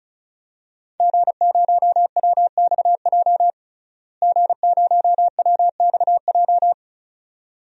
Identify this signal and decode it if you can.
Morse code, heard as a keyed tone, about 36 wpm, 700 Hz. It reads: G0WXJ G0WXJ
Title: DXCC-100 Call Signs - 5 Characters - Repeated 2x - 35wpm